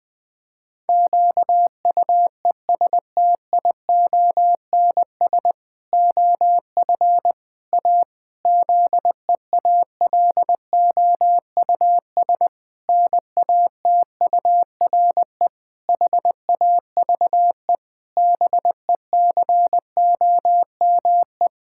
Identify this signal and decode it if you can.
Morse code, heard as a keyed tone, about 20 wpm, 700 Hz. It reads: QUESTIONS OF A ZEALOUS NATURE HAVE BECOME
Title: Pangram Challenge v2 - 20wpm